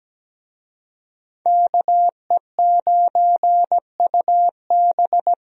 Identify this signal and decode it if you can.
Morse code, heard as a keyed tone, about 17 wpm, 700 Hz. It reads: KE9UB
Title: USA Call Signs - 5 Characters - 17wpm